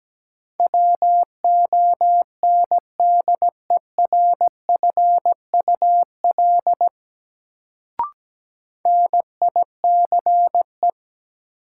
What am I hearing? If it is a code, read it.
Morse code, heard as a keyed tone, about 17 wpm, 700 Hz. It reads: WONDERFUL NICE
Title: Sentences from Top 1100 Words - New Word Drill - 17wpm